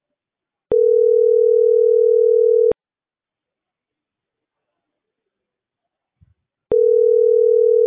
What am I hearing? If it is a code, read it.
Ringback tone (North America), 440 + 480 Hz.